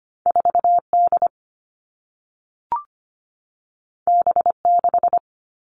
Morse code, heard as a keyed tone, about 25 wpm, 700 Hz. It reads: 4 D B 6
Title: All Cut Numbers - 2-Digit 25wpm